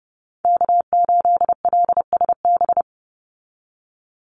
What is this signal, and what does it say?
Morse code, heard as a keyed tone, about 30 wpm, 700 Hz. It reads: K8LSB
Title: Call Signs Repeated 3x - 30wpm